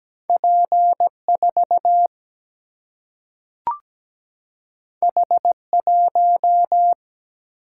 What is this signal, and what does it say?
Morse code, heard as a keyed tone, about 17 wpm, 700 Hz. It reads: P 4 H 1
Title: Single-Letter Single-Number 17wpm